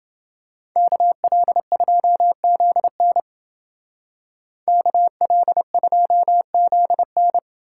Morse code, heard as a keyed tone, about 30 wpm, 700 Hz. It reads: KL2ZN KL2ZN